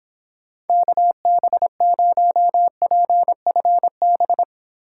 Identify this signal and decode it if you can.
Morse code, heard as a keyed tone, about 29 wpm, 700 Hz. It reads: KB0PFB